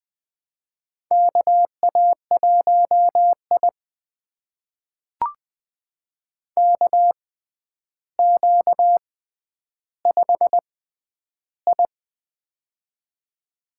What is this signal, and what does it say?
Morse code, heard as a keyed tone, about 21 wpm, spaced as for 10 wpm, 700 Hz. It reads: KA1I KQ5I
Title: USA Call Signs - 4 Characters - 6x Character Spacing - 20wpm